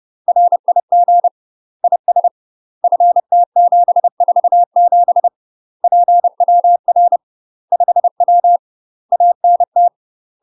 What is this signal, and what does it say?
Morse code, heard as a keyed tone, about 30 wpm, 700 Hz. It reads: RIG IS FT747 PWR 5W ANT
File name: UK MASTER QSOs TRAINING / 30WPM / UK MASTER QSOs 0041